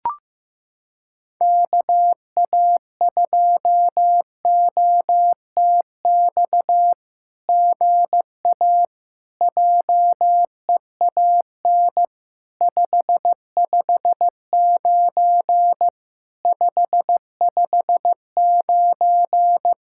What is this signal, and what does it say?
Morse code, heard as a keyed tone, about 15 wpm, 700 Hz. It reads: KA2OTX GA JEAN 559 559
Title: SOTA QSO Practice 15wpm